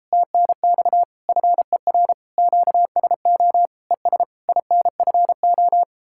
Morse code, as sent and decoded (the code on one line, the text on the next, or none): - -. -..- / ..-. .-. / --.- ... --- / . ... / .. -. ..-. ---
TNX FER QSO ES INFO